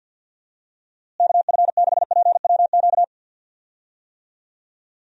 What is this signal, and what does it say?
Morse code, heard as a keyed tone, about 49 wpm, 700 Hz. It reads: KF6PWQ
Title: USA Call Signs - 6 Characters - Repeated 3x - 50wpm